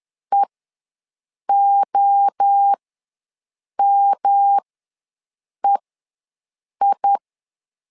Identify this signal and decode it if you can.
Morse code, heard as a keyed tone, about 10 wpm, spaced as for 6 wpm, 800 Hz. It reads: EOMEI
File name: LEZIONI / 3 MWRO / 3 MWRO ws 10wpm 2310